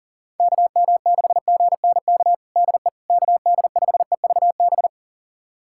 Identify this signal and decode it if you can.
Morse code, heard as a keyed tone, about 37 wpm, 700 Hz. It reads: KM6GNK DE KD5EVB
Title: USA Callsign DE Callsign - 6 Characters - 40wpm